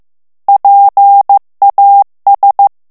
Morse code, heard as a keyed tone, about 15 wpm, 800 Hz. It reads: PAS
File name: FRANCAIS TEXTE CLAIR 2 VITESSE 14 MOTS MINUTES